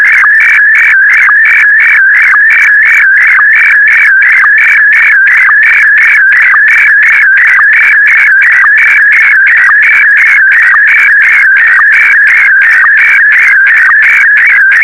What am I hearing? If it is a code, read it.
SSTV (slow-scan television), Scottie DX.
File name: sstv / SDX-VK4MSL